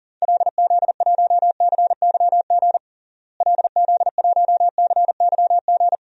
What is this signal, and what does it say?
Morse code, heard as a keyed tone, about 40 wpm, 700 Hz. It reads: LZ1CYG LZ1CYG
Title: DXCC-100 Call Signs - 6 Characters - Repeated 2x - 40wpm